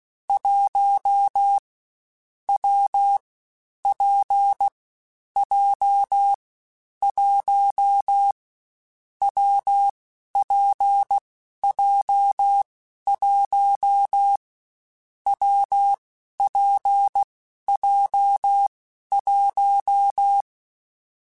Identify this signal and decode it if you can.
Morse code, heard as a keyed tone, about 16 wpm, 800 Hz. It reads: 1 W P J 1 W P J 1 W P J 1
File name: Difficult Combos / Difficult Combos-mp3s / WPJ WPJ1 @15